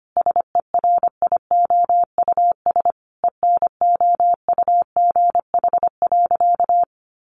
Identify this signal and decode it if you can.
Morse code, heard as a keyed tone, about 25 wpm, 700 Hz. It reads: SERIOUS ENOUGH.
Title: Sentences from Top 1100 Words 25wpm